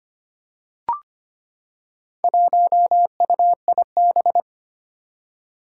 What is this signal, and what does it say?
Morse code, heard as a keyed tone, about 25 wpm, 700 Hz. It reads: 1UIB
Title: Four Challenging Letter-Number Combinations 25wpm